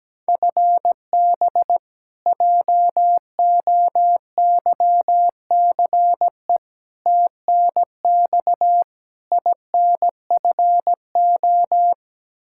Morse code, heard as a keyed tone, about 17 wpm, 700 Hz. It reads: FB JOYCE TNX INFO
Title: K1USN SST QSO Practice 17wpm